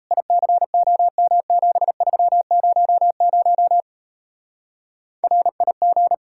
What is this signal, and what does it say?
Morse code, heard as a keyed tone, about 35 wpm, 700 Hz. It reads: ICOM7300 RIG
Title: Speed-Racing Common QSO Phrases - 28wpm